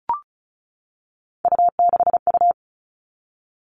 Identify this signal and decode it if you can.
Morse code, heard as a keyed tone, about 34 wpm, 700 Hz. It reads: U6U
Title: All Cut Numbers - 3-Digit 35wpm